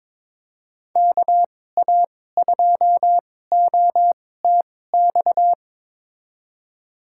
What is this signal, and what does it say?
Morse code, heard as a keyed tone, about 23 wpm, spaced as for 16 wpm, 700 Hz. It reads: KA2OTX
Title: USA Call Signs - 2x Character Spacing - 22wpm